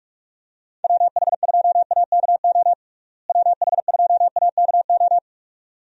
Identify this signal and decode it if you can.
Morse code, heard as a keyed tone, about 41 wpm, 700 Hz. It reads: WH2AKO WH2AKO
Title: USA Call Signs - 6 Characters - Repeated 2x - 45wpm